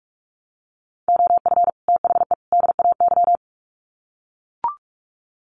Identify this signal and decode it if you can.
Morse code, heard as a keyed tone, about 44 wpm, 700 Hz. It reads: OF THE DAY
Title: Sets of 3 Words 45wpm